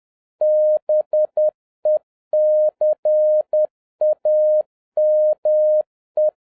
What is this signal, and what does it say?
Morse code, heard as a keyed tone, about 10 wpm, 610 Hz. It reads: BECAME
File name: 2011-07-mp3 / n31 THE 3.11 2 10wpm